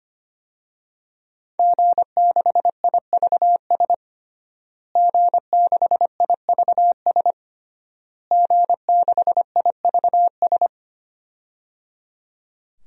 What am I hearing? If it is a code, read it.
Morse code, heard as a keyed tone, about 25 wpm, 700 Hz. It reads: G6IVS G6IVS G6IVS
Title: DXCC-100 Call Signs - 5 Characters - Repeated 3x - 25wpm